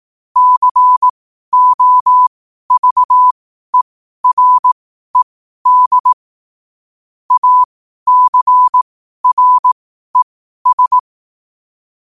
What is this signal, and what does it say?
Morse code, heard as a keyed tone, about 18 wpm, spaced as for 12 wpm, 1000 Hz. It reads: COVERED ACRES
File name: qotdfiles / 12wpm